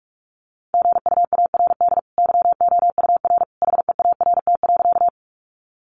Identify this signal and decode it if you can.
Morse code, heard as a keyed tone, about 44 wpm, 700 Hz. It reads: GUARD YOUR HEART.
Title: Sentences from Top 1200 Words 45wpm